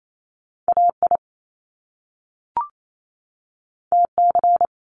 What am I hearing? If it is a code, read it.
Morse code, heard as a keyed tone, about 28 wpm, 700 Hz. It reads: AI TC
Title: Top Two Letter Combinations in Words 28wpm